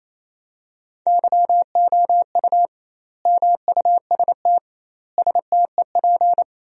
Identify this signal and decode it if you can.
Morse code, heard as a keyed tone, about 27 wpm, 700 Hz. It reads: YOU MUST STEP
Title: 2x Word-Spacing - Sentences from Top 400 Words - 28wpm